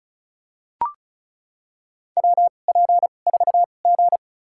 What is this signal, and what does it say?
Morse code, heard as a keyed tone, about 36 wpm, spaced as for 25 wpm, 700 Hz. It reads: WP4G
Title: USA Call Signs - 4 Characters - 2x Character Spacing - 35wpm